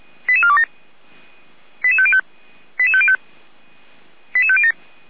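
Five-tone selective call (ZVEI), heard as quick runs of five tones, tones of about 70 ms. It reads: 88427 88505 88505 88507
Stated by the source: Five-tone ZVEI selective call received on an SDR.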